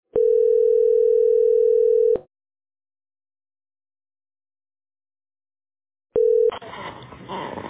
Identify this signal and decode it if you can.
Ringback tone (North America), 440 + 480 Hz.